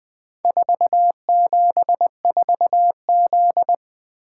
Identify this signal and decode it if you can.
Morse code, heard as a keyed tone, about 20 wpm, 700 Hz. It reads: 474Z
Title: Four Letter-Number Combinations - 20wpm